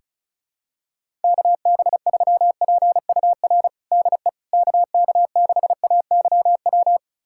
Morse code, heard as a keyed tone, about 36 wpm, 700 Hz. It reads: KB3PUR DE KK6AYW
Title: USA Callsign DE Callsign - 6 Characters - 35wpm